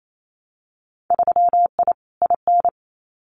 Morse code, heard as a keyed tone, about 28 wpm, 700 Hz. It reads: 3I IN
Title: Winter Field Day Exchanges 28wpm